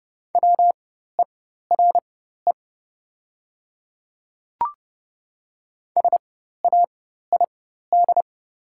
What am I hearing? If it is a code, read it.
Morse code, heard as a keyed tone, about 30 wpm, spaced as for 14 wpm, 690 Hz. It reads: WERE SAID
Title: 4x Character Spacing - Top 200 Words - 30wpm